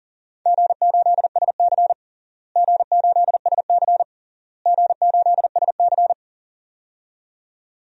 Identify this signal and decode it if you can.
Morse code, heard as a keyed tone, about 37 wpm, 700 Hz. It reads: G8SC G8SC G8SC
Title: DXCC-100 Call Signs - 4 Characters - Repeated 3x - 40wpm